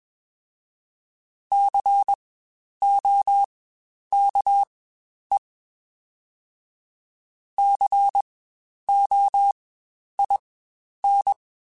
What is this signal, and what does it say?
Morse code, heard as a keyed tone, about 20 wpm, spaced as for 10 wpm, 800 Hz. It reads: COKE COIN
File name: Spaced Alphas / Spaced Alphas mp3s / Cs sp @20